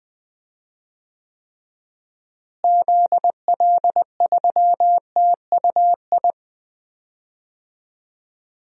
Morse code, heard as a keyed tone, about 20 wpm, 700 Hz. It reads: ZL3TUI
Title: DXCC-100 Call Signs - 6 Characters - Repeated 2x - 20wpm